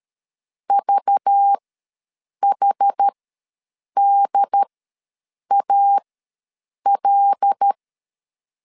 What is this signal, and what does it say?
Morse code, heard as a keyed tone, about 13 wpm, spaced as for 7 wpm, 800 Hz. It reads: VHDAL
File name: LEZIONI / 5 LBVC / 5 LBV ws 12wpm 3015